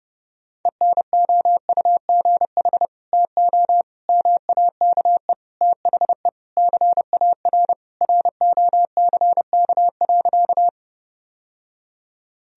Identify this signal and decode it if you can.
Morse code, heard as a keyed tone, about 30 wpm, 700 Hz. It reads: ENOUGH TO MAKE THE CAR ROCK.
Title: Sentences from Top 1000 Words - Review Part 1 - 30wpm